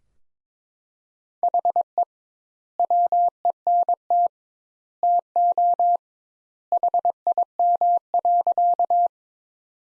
Morse code, heard as a keyed tone, about 21 wpm, 700 Hz. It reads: HE WENT TO HIM.
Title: Sentences Top 200 Words - 22wpm